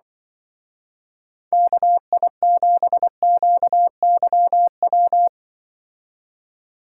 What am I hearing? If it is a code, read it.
Morse code, heard as a keyed tone, about 25 wpm, 700 Hz. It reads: KI7QYW
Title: Speed Racing - USA 2x3 Call Signs 20wpm